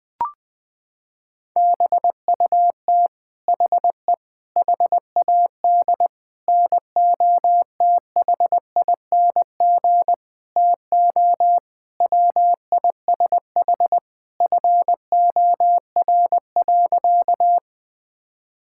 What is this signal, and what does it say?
Morse code, heard as a keyed tone, about 20 wpm, 700 Hz. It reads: BUT HE HAD NOTHING TO WISH FOR.
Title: Sentences from Top 1600 Words 20wpm